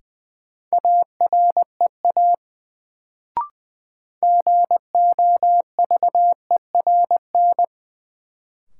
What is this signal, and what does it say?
Morse code, heard as a keyed tone, about 20 wpm, 700 Hz. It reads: AREA GOVERN
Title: Sentences from Top 1000 Words - Review Part 2 - New Word Drill - 20wpm